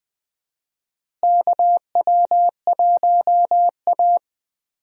Morse code, heard as a keyed tone, about 20 wpm, 700 Hz. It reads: KW1A
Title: USA Call Signs - 4 Characters - Repeated 2x - 20wpm